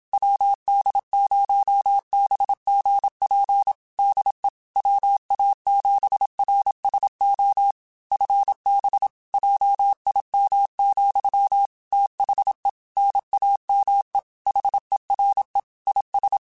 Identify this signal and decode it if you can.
Morse code, heard as a keyed tone, about 27 wpm, 800 Hz. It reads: WD0BGP DE WA7RSO FB JIM, THE NAME HERE IS